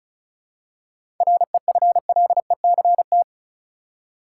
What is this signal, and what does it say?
Morse code, heard as a keyed tone, about 34 wpm, 700 Hz. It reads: REFLECT
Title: Seven Letter Words - 35wpm